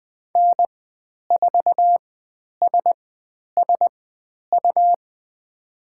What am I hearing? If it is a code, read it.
Morse code, heard as a keyed tone, about 20 wpm, spaced as for 10 wpm, 700 Hz. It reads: N4SSU
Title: 102 Lesson 17g - Letter u - Call Signs 20wpm